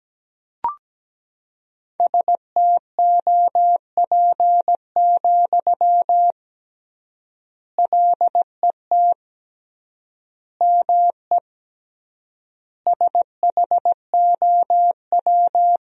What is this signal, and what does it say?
Morse code, heard as a keyed tone, about 17 wpm, 700 Hz. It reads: STOP, LET ME SHOW
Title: 3x Word-Spacing - Sentences from Top 300 Words - 17wpm